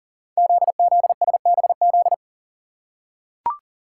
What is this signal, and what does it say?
Morse code, heard as a keyed tone, about 38 wpm, 690 Hz. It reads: ZZSBZ
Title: Code Groups - 5 Letters - 40wpm